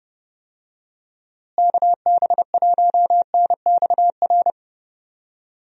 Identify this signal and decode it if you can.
Morse code, heard as a keyed tone, about 30 wpm, 700 Hz. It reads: KB1NXR
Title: USA Call Signs - 6 Characters - Repeated 3x - 30wpm